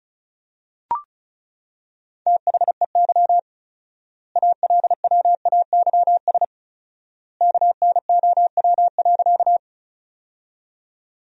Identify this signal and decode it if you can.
Morse code, heard as a keyed tone, about 36 wpm, 700 Hz. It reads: THEY ALWAYS KNOW.
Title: 4x Word-Spacing - Sentences from Top 300 Words - 35wpm